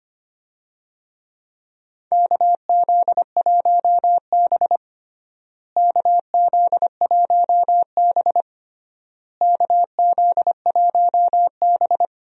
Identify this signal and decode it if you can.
Morse code, heard as a keyed tone, about 25 wpm, 700 Hz. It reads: KZ1B KZ1B KZ1B
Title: USA Call Signs - 4 Characters - Repeated 3x - 25wpm